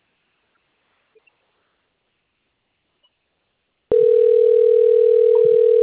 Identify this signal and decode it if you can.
Ringback tone (North America), 440 + 480 Hz.
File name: out- - - - - .12395